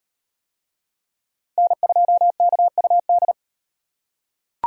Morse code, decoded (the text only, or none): N2KUD